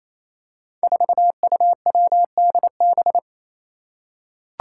Morse code, heard as a keyed tone, about 29 wpm, 700 Hz. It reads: 4UWDB
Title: Five Challenging Letter-Number Combinations 28wpm